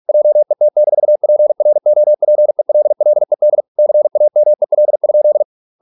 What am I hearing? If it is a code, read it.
Morse code, heard as a keyed tone, about 46 wpm, 600 Hz. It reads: JET-PROPELLED CAMEL?